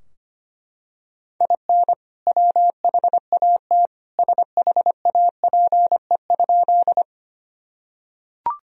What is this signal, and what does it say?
Morse code, heard as a keyed tone, about 25 wpm, 700 Hz. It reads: IN WHAT SHAPE?
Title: Sentences from Top 1000 Words - Review Part 3 - 25wpm